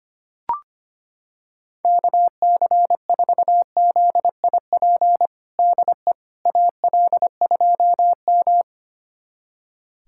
Morse code, heard as a keyed tone, about 25 wpm, 700 Hz. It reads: KC4ZIP DE AL2M
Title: USA Callsign DE Callsign 25wpm